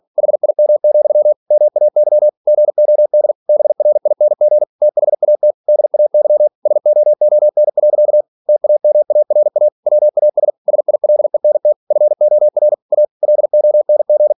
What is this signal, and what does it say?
Morse code, heard as a keyed tone, about 46 wpm, 600 Hz. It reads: HIM, MAY GOD BRING THAT DAY SOON. TAMARA WAS SILENT FOR A LONG
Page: Never 47wpm